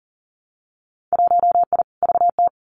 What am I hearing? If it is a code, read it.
Morse code, heard as a keyed tone, about 39 wpm, 700 Hz. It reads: 1I VT
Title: Winter Field Day Exchanges Encore 40wpm